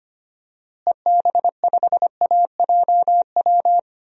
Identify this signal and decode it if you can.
Morse code, heard as a keyed tone, about 25 wpm, 700 Hz. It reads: EB5AJW
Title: DXCC-100 Call Signs - 6 Characters - Repeated 3x - 25wpm